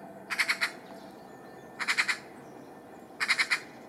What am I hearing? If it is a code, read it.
Emergency Alert System (SAME) tones, heard as harsh buzzing bursts. It is heard from 0.3 to 3.6 s.